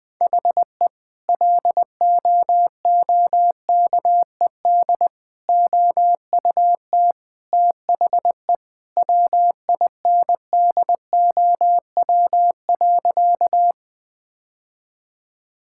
Morse code, heard as a keyed tone, about 21 wpm, 700 Hz. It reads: HE LOOKED OUT THE WINDOW.
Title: Sentences with 2-Word Phrases - Spoken Beginning - 20wpm